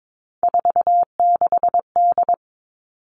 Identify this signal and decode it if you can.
Morse code, heard as a keyed tone, about 22 wpm, 700 Hz. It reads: 46D
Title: Three Challenging Letter-Number Combinations 22wpm